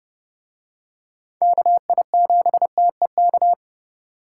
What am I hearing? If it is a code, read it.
Morse code, heard as a keyed tone, about 27 wpm, 700 Hz. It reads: KI7TEK